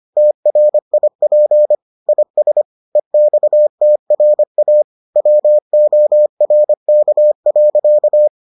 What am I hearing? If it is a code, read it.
Morse code, heard as a keyed tone, about 25 wpm, 600 Hz. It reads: TRIP IS EXTRA WORK.